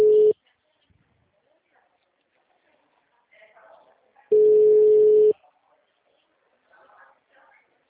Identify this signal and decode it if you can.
Ringback tone (Europe, 425 Hz), with speech.